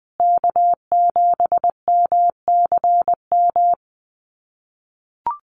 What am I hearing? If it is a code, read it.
Morse code, heard as a keyed tone, about 20 wpm, 700 Hz. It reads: K7MCM
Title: USA Call Signs Repeated 3x 20wpm